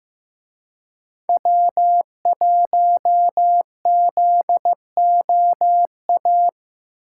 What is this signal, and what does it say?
Morse code, heard as a keyed tone, about 17 wpm, 700 Hz. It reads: W1ZOA
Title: Speed Racing - USA 1x3 Call Signs 15wpm